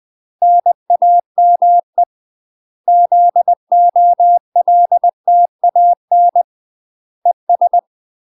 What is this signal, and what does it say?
Morse code, heard as a keyed tone, about 20 wpm, 700 Hz. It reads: NAME ZOLTAN ES
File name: UK MASTER QSOs TRAINING / 20WPM / UK MASTER QSOs 0039